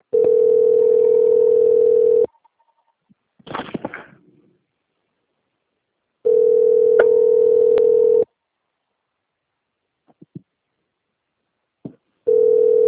Ringback tone (North America), 440 + 480 Hz.
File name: out- - - - - .52302